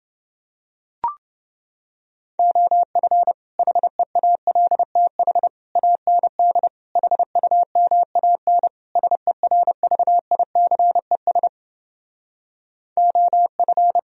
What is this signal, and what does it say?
Morse code, heard as a keyed tone, about 29 wpm, 700 Hz. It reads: OF HEALTH AND HUMAN SERVICES OF